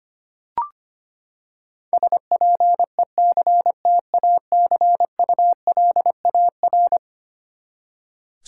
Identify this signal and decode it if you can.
Morse code, heard as a keyed tone, about 25 wpm, 700 Hz. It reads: SPECTACULAR
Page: morse-code-ninja-practice-25wpm